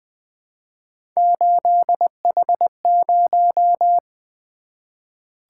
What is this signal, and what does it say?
Morse code, heard as a keyed tone, about 20 wpm, 700 Hz. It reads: 8H0
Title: Three Letter-Number Combinations - 20wpm